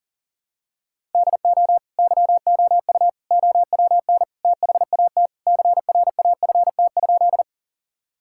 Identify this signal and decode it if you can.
Morse code, heard as a keyed tone, about 40 wpm, 700 Hz. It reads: DO YOU OWN THAT CRAFT?
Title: Sentences from Top 1000 Words - Review Part 6 - 40wpm